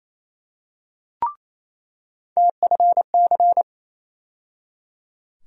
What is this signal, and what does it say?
Morse code, heard as a keyed tone, about 28 wpm, 700 Hz. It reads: TFC
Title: USA Call Sign Postfixes - 3 Letters 28wpm